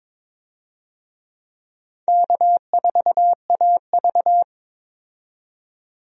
Morse code, heard as a keyed tone, about 22 wpm, 700 Hz. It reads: K4AV
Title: USA Call Signs - 4 Characters - 22wpm